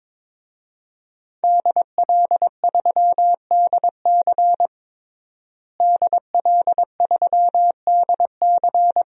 Morse code, heard as a keyed tone, about 23 wpm, 700 Hz. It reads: DL3DC DL3DC
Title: DXCC-100 Call Signs - 5 Characters - Repeated 3x - 22wpm